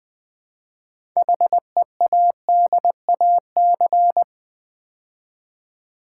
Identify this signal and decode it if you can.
Morse code, heard as a keyed tone, about 20 wpm, 700 Hz. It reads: HEADAC
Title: Build A Word - 3 Letters At A Time - 8-Letter Words - 20wpm